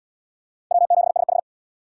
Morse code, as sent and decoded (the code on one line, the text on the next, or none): .-- --... ... ....
W7SH